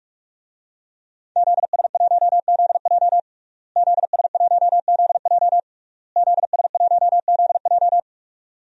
Morse code, heard as a keyed tone, about 41 wpm, 700 Hz. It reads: ZS1ZJ ZS1ZJ ZS1ZJ
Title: DXCC-100 Call Signs - 5 Characters - Repeated 3x - 45wpm